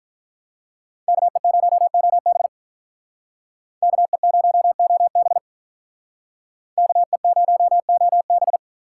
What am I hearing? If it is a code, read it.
Morse code, heard as a keyed tone, about 46 wpm, 700 Hz. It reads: KE0OB KE0OB KE0OB